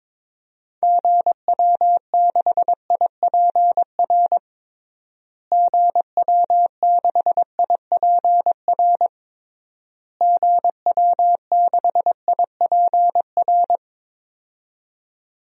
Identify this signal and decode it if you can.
Morse code, heard as a keyed tone, about 22 wpm, 700 Hz. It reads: GW6IPR GW6IPR GW6IPR
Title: DXCC-100 Call Signs - 6 Characters - Repeated 3x - 22wpm